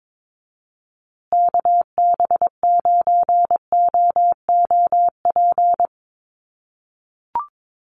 Morse code, heard as a keyed tone, about 22 wpm, 700 Hz. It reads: KB9OOP
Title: USA Call Signs Repeated 3x 22wpm